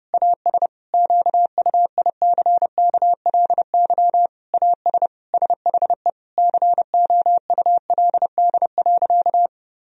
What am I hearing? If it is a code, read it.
Morse code, heard as a keyed tone, about 32 wpm, 700 Hz. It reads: AS QUICKLY AS SHE COULD.
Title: Sentences from Top 1000 Words - Review Part 3 - 30wpm